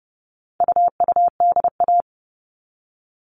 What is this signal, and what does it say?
Morse code, heard as a keyed tone, about 30 wpm, 700 Hz. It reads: UUDA
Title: All Cut Numbers - 4-Digit 30wpm